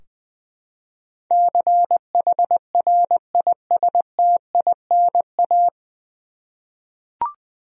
Morse code, heard as a keyed tone, about 21 wpm, 700 Hz. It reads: CHRISTINA
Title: USA Common Names - 20wpm